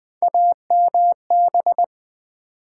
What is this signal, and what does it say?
Morse code, heard as a keyed tone, about 20 wpm, 700 Hz. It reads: AMB
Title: Build A Word - 3 Letters At A Time - 8-Letter Words - 20wpm